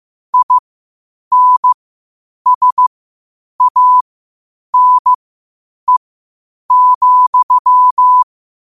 Morse code, heard as a keyed tone, about 15 wpm, spaced as for 8 wpm, 1000 Hz. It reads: INSANE,
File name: qotdfiles / 8wpm